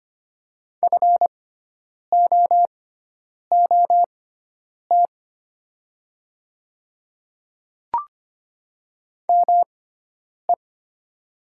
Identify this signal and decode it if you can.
Morse code, heard as a keyed tone, about 25 wpm, spaced as for 9 wpm, 700 Hz. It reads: FOOT ME
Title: 6x Character Spacing - Top 500 Words - 25wpm